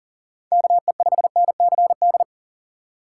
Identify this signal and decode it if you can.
Morse code, heard as a keyed tone, about 40 wpm, 700 Hz. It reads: KE5NCD
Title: USA Call Signs - 6 Characters - 2x Character Spacing - 40wpm